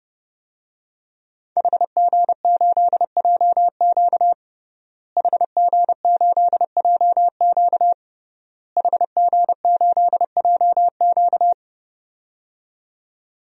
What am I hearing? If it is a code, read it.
Morse code, heard as a keyed tone, about 30 wpm, 700 Hz. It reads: HG8JQ HG8JQ HG8JQ
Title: DXCC-100 Call Signs - 5 Characters - Repeated 3x - 30wpm